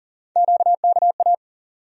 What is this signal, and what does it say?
Morse code, heard as a keyed tone, about 39 wpm, 700 Hz. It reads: QKA
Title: USA Call Sign Postfixes - 3 Letters 40wpm